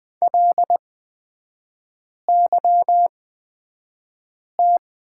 Morse code, heard as a keyed tone, about 20 wpm, spaced as for 5 wpm, 700 Hz. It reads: LYT
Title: 243 Lesson 39e - Letter z - Three Characters 20wpm